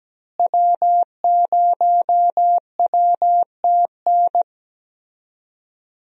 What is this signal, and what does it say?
Morse code, heard as a keyed tone, about 17 wpm, 700 Hz. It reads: W0WTN
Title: USA Call Signs - 5 Characters - Repeated 3x - 17wpm